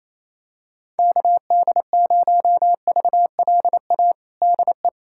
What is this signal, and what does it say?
Morse code, heard as a keyed tone, about 28 wpm, 700 Hz. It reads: KD0VLA DE
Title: USA Callsign DE Callsign - 6 Characters - 28wpm